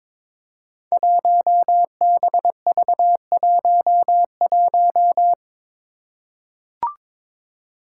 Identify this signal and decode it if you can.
Morse code, heard as a keyed tone, about 22 wpm, 700 Hz. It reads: 1BV11
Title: Code Groups - 5 Letter:Numbers - 22wpm